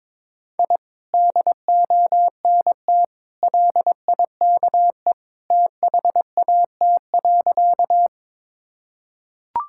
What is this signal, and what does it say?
Morse code, heard as a keyed tone, about 21 wpm, 700 Hz. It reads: I DONT LIKE THAT.
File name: practice / 22wpm / Sentences with 2-Word Phrases - 22wpm